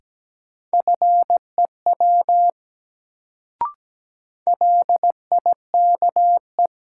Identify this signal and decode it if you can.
Morse code, heard as a keyed tone, about 17 wpm, 700 Hz. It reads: FEW LIKE
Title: Sets of 2 Words - New Word Drill - 17wpm